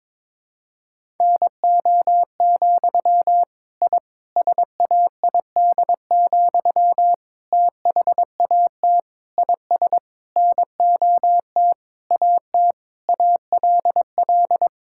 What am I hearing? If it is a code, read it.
Morse code, heard as a keyed tone, about 22 wpm, 700 Hz. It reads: NO, I SAID, THAT IS NOT AT ALL
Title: Sentences from Top 1000 Words - Review Part 6 - 22wpm